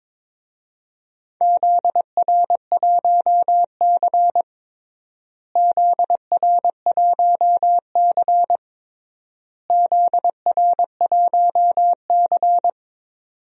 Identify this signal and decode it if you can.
Morse code, heard as a keyed tone, about 22 wpm, 700 Hz. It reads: ZR1C ZR1C ZR1C
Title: DXCC-100 Call Signs - 4 Characters - Repeated 3x - 22wpm